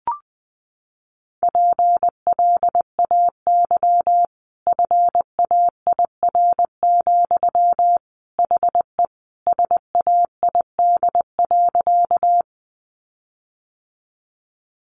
Morse code, heard as a keyed tone, about 20 wpm, 700 Hz. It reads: PLAY FAIR, HE SAID.
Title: Sentences from Top 1300 Words 20wpm